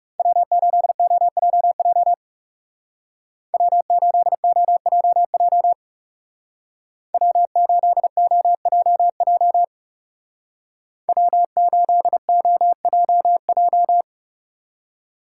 Morse code, heard as a keyed tone, about 36 wpm, 700 Hz. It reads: W8OJJ W8OJJ W8OJJ W8OJJ
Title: Speed Racing - USA 1x3 Call Signs 30wpm